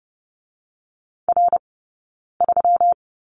Morse code, heard as a keyed tone, about 30 wpm, spaced as for 9 wpm, 700 Hz. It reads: R3
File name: Lesson 20d - Number 3 - Two Characters 30wpm